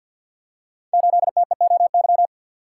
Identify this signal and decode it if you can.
Morse code, heard as a keyed tone, about 52 wpm, 700 Hz. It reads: 8TEOY
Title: Code Groups - 5 Letter:Numbers - 50wpm